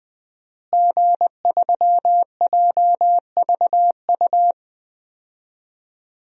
Morse code, heard as a keyed tone, about 21 wpm, 700 Hz. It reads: G3JVU
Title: DXCC-100 Call Signs - 5 Characters - Repeated 2x - 20wpm